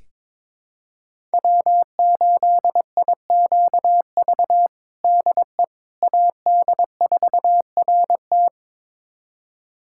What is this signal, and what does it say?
Morse code, heard as a keyed tone, about 22 wpm, 700 Hz. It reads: W8IQV DE AD4RT
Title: USA Callsign DE Callsign 22wpm